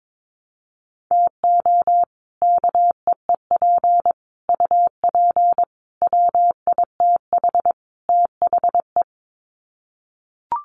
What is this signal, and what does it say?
Morse code, heard as a keyed tone, about 22 wpm, 700 Hz. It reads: TO KEEP UP WITH THE
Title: Sets of 5 Words Encore 22wpm